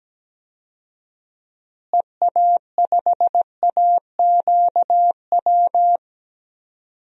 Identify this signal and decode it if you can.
Morse code, heard as a keyed tone, about 17 wpm, 700 Hz. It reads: EA5AQW
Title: DXCC-100 Call Signs - 6 Characters - Repeated 3x - 17wpm